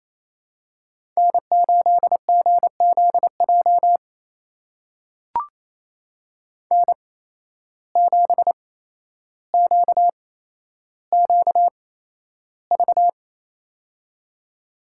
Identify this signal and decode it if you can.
Morse code, heard as a keyed tone, about 28 wpm, spaced as for 12 wpm, 700 Hz. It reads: N8GZJ N7QQV
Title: USA Call Signs - 5 Characters - 8x Character Spacing - 28wpm